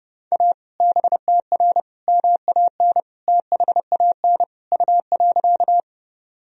Morse code, heard as a keyed tone, about 30 wpm, 700 Hz. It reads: A BTR MAN THAN U.
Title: Sentences from Top 400 Words and Abbreviations 30wpm